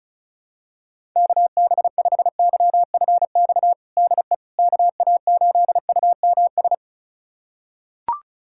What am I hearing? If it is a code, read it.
Morse code, heard as a keyed tone, about 36 wpm, 700 Hz. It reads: KB5YFX DE KA8UMS
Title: USA Callsign DE Callsign - 6 Characters - 35wpm